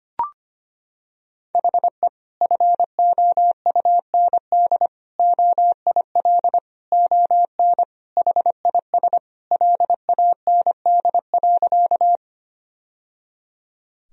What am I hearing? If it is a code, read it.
Morse code, heard as a keyed tone, about 25 wpm, 700 Hz. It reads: HE FOUND OIL ON HIS LAND.
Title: Sentences from Top 1000 Words - Review Part 4 - 25wpm